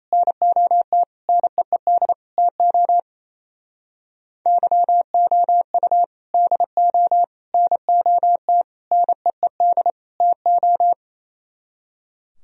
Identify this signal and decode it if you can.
Morse code, heard as a keyed tone, about 36 wpm, 700 Hz. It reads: NOT NEED TO YOU DO NOT NEED TO